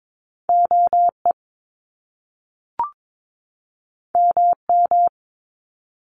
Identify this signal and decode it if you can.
Morse code, heard as a keyed tone, about 22 wpm, 700 Hz. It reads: OE MM
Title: Top Two Letter Combinations in Words 22wpm